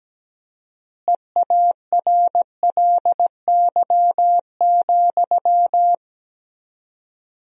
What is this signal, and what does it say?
Morse code, heard as a keyed tone, about 17 wpm, 700 Hz. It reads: EARLY,
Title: 4x Word-Spacing - Sentences from Top 400 Words - 17wpm